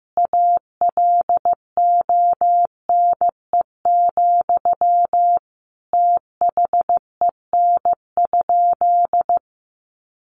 Morse code, heard as a keyed tone, about 15 wpm, 700 Hz. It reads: ALONE, THEN?
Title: Sentences from Top 1000 Words 15wpm